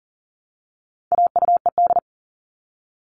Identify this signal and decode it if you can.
Morse code, heard as a keyed tone, about 39 wpm, 700 Hz. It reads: AUED
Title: All Cut Numbers - 4-Digit 40wpm